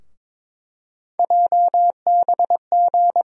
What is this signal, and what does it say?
Morse code, heard as a keyed tone, about 22 wpm, 700 Hz. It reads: JBG
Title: USA Call Sign Postfixes - 3 Letters 22wpm